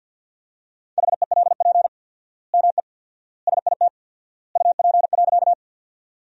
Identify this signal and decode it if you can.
Morse code, heard as a keyed tone, about 50 wpm, 700 Hz. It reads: HELP ME SIT UP.
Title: 4x Word-Spacing - Sentences from Top 500 Words - 50wpm